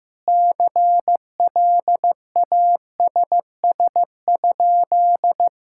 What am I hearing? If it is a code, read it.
Morse code, heard as a keyed tone, about 15 wpm, 700 Hz. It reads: CLASS?
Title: Sentences from Top 1000 Words - Review Part 5 - 15wpm